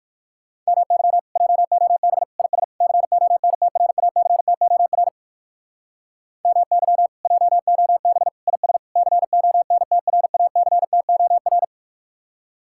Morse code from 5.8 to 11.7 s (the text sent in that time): MY JOB IS CONTRACTOR